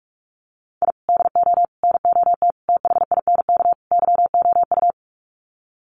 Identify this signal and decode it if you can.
Morse code, heard as a keyed tone, about 44 wpm, 700 Hz. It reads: I DO NOT THINK YOU
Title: Sets of 5 Words 45wpm